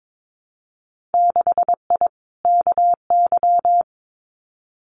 Morse code, heard as a keyed tone, about 22 wpm, 700 Hz. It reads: 6I KY
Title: Winter Field Day Exchanges Encore 22wpm